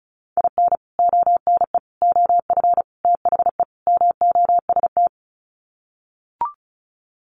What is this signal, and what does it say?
Morse code, heard as a keyed tone, about 34 wpm, 700 Hz. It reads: IN ONE OF THE MOST
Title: Sets of 5 Words Encore 35wpm